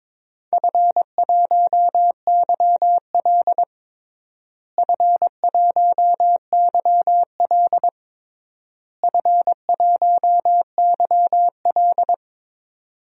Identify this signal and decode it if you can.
Morse code, heard as a keyed tone, about 22 wpm, 700 Hz. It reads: F1YL F1YL F1YL
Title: DXCC-100 Call Signs - 4 Characters - Repeated 3x - 22wpm